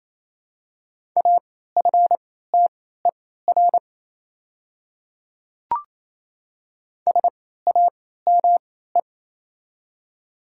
Morse code, heard as a keyed tone, about 28 wpm, spaced as for 16 wpm, 700 Hz. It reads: AFTER SAME
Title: 3x Character Spacing - Top 200 Words - 28wpm